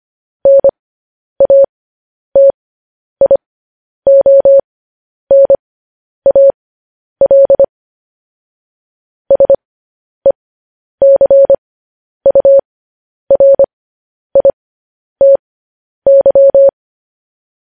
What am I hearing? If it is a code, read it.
Morse code, heard as a keyed tone, about 25 wpm, spaced as for 10 wpm, 550 Hz. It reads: NATIONAL SECURITY
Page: news headlines in morse code at 10 wpm